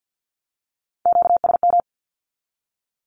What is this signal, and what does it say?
Morse code, heard as a keyed tone, about 50 wpm, 700 Hz. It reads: QSM
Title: Q-Codes with Spoken Meaning - 50wpm